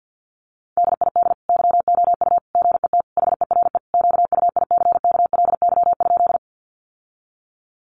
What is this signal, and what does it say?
Morse code, heard as a keyed tone, about 48 wpm, 700 Hz. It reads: DID YOU GET HERE QUICKLY?
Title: Sentences from Top 700 Words 50wpm